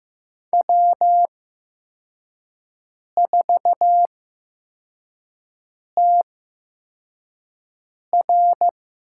Morse code, heard as a keyed tone, about 15 wpm, spaced as for 4 wpm, 700 Hz. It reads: W4TR
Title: USA Call Signs - 4 Characters - 8x Character Spacing - 15wpm